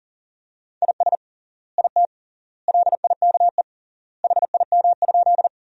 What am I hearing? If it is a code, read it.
Morse code, heard as a keyed tone, about 40 wpm, 700 Hz. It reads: IS IT LIKE HIM?
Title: 3x Word-Spacing - Sentences from Top 100 Words - 40wpm